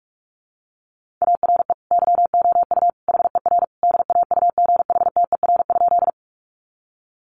Morse code, heard as a keyed tone, about 44 wpm, 700 Hz. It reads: ARE YOU HER DAUGHTER?
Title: Sentences from Top 1500 Words 45wpm